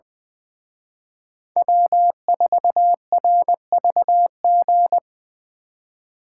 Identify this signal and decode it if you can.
Morse code, heard as a keyed tone, about 25 wpm, 700 Hz. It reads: W4RVG
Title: Speed Racing - USA 1x3 Call Signs 20wpm